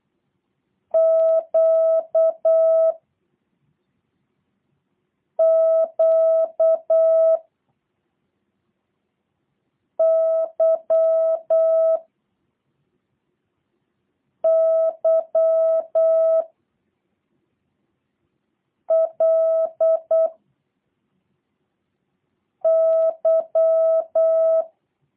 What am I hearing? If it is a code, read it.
Morse code, heard as a keyed tone, about 8 wpm, 650 Hz. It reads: Q Q Y Y L Y
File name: jh8cbh / 2023cw / review1-6